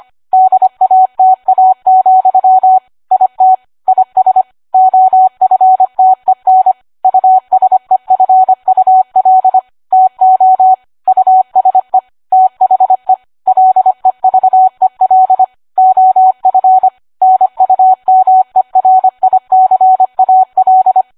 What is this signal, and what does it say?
Morse code, heard as a keyed tone, about 25 wpm, 750 Hz. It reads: DATA, IT IS OFTEN USEFUL TO USE THE LEVEL OF NUMERICAL